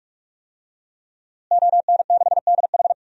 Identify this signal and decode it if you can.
Morse code, heard as a keyed tone, about 46 wpm, 700 Hz. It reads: ON6DH
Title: DXCC-100 Call Signs - 5 Characters - Repeated 3x - 45wpm